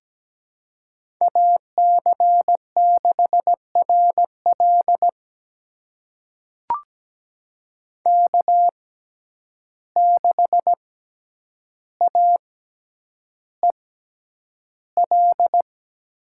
Morse code, heard as a keyed tone, about 17 wpm, spaced as for 9 wpm, 700 Hz. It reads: AC6RL K6AEL
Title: USA Call Signs - 5 Characters - 6x Character Spacing - 17wpm